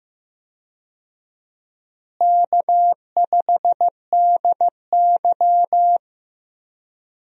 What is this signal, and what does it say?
Morse code, heard as a keyed tone, about 15 wpm, 700 Hz. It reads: K5DY
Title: USA Call Signs - 4 Characters - Repeated 3x - 15wpm